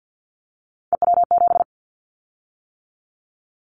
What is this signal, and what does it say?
Morse code, heard as a keyed tone, about 50 wpm, 700 Hz. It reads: EW7
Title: Callsign Prefixes for Top 100 DXCC - 50wpm